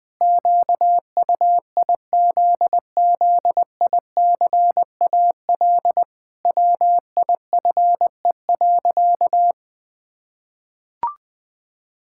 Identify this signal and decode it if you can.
Morse code, heard as a keyed tone, about 20 wpm, 700 Hz. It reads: QUIZZICAL WIFE.
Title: Pangram Challenge v2 - 20wpm